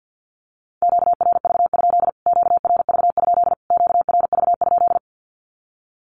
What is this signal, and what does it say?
Morse code, heard as a keyed tone, about 50 wpm, 700 Hz. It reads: QRV? QRV? QRV?
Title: Q-Codes Repeated 3x 50wpm